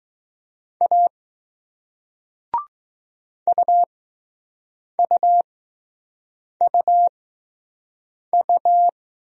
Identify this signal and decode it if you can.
Morse code, heard as a keyed tone, about 21 wpm, spaced as for 6 wpm, 700 Hz. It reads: A UUUU